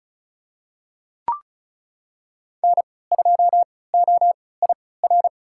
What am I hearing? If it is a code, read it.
Morse code, heard as a keyed tone, about 36 wpm, spaced as for 20 wpm, 700 Hz. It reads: N2OIR
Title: USA Call Signs - 3x Character Spacing - 35wpm